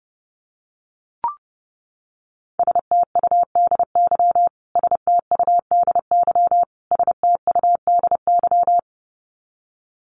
Morse code, heard as a keyed tone, about 30 wpm, 700 Hz. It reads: STUDY STUDY STUDY
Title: Top 500 Words Repeated 30wpm